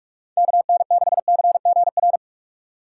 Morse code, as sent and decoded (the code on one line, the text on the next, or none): -.- -. -.... -.-. --. .-.
KN6CGR